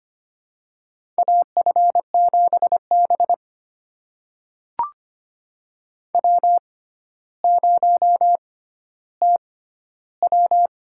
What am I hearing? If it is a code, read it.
Morse code, heard as a keyed tone, about 25 wpm, spaced as for 13 wpm, 700 Hz. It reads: AF7B W0TW
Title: USA Call Signs - 6x Character Spacing - 25wpm